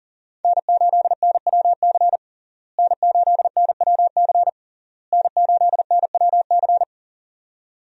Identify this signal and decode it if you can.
Morse code, heard as a keyed tone, about 40 wpm, 700 Hz. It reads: N8NWC N8NWC N8NWC
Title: USA Call Signs - 5 Characters - Repeated 3x - 40wpm